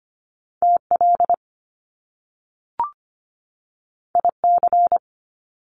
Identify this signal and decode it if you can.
Morse code, heard as a keyed tone, about 25 wpm, 700 Hz. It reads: TL IC
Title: Top Two Letter Combinations in Words 25wpm